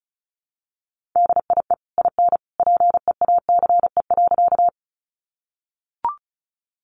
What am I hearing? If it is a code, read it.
Morse code, heard as a keyed tone, about 34 wpm, 700 Hz. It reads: DIE IN PEACE.
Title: Sentences from Top 1500 Words 35wpm